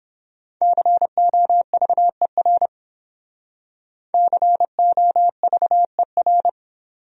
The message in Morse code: COVER COVER

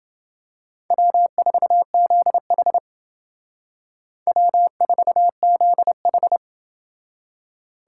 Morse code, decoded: W4ZH W4ZH

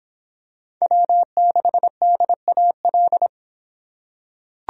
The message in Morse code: W6DAL

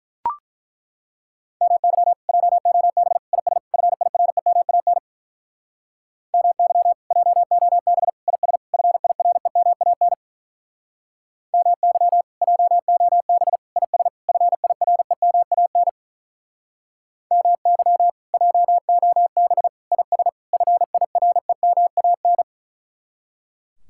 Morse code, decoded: MY JOB IS FIREMAN MY JOB IS FIREMAN MY JOB IS FIREMAN MY JOB IS FIREMAN

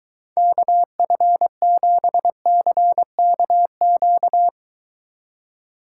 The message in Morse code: KF7CKQ

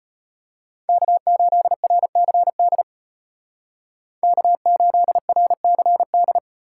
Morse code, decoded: K8RCD K8RCD

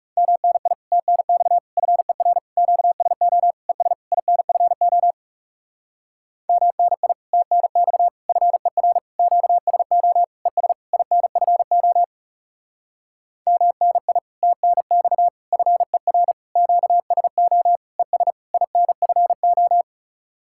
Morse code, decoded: MNI TNX FER QSO ES INFO MNI TNX FER QSO ES INFO MNI TNX FER QSO ES INFO

